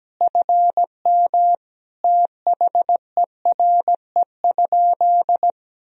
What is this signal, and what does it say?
Morse code, heard as a keyed tone, about 17 wpm, 700 Hz. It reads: FM THERE?
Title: Sentences from Top 100 Words and Abbreviations 17wpm